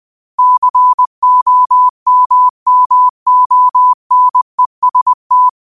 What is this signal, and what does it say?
Morse code, heard as a keyed tone, about 20 wpm, 1000 Hz. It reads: COMMONEST